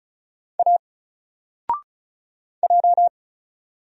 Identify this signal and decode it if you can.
Morse code, heard as a keyed tone, about 35 wpm, 700 Hz. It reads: A J
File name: Single-Letter-Number-35wpm